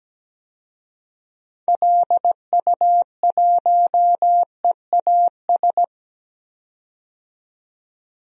Morse code, heard as a keyed tone, about 17 wpm, 700 Hz. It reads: LU1EAS
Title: DXCC-100 Call Signs - 6 Characters - 17wpm